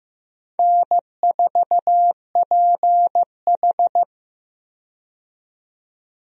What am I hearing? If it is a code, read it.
Morse code, heard as a keyed tone, about 15 wpm, 700 Hz. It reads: N4PH
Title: USA Call Signs - 4 Characters - 15wpm